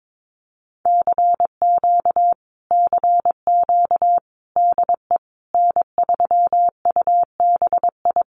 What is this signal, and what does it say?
Morse code, heard as a keyed tone, about 22 wpm, 700 Hz. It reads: CQ CQ DE N3UBI
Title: Common QSO Phrases Encore 22wpm